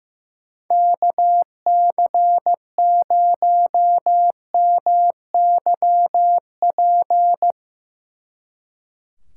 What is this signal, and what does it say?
Morse code, heard as a keyed tone, about 16 wpm, 700 Hz. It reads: KC0MYP
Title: Speed Racing - USA 2x3 Call Signs 15wpm